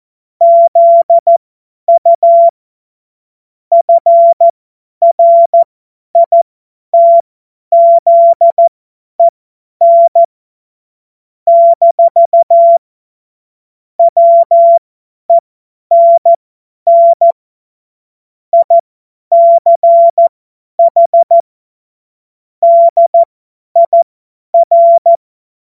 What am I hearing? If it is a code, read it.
Morse code, heard as a keyed tone, about 14 wpm, spaced as for 10 wpm, 680 Hz. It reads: ZU FRITZEN - WENN ICH DIR
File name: Morsekurs / Witze / Witz 17. August 2021, 50 F70